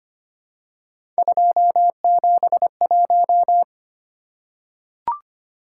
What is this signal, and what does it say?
Morse code, heard as a keyed tone, about 25 wpm, 700 Hz. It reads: 271
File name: Three Letter-Number Combinations - 25wpm